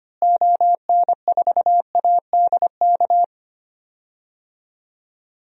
Morse code, heard as a keyed tone, about 25 wpm, 700 Hz. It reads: ON4ADK
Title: DXCC-100 Call Signs - 6 Characters - Repeated 3x - 25wpm